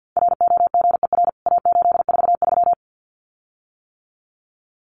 Morse code, heard as a keyed tone, about 50 wpm, 700 Hz. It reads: ROGER A843
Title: CWT Exchange 50wpm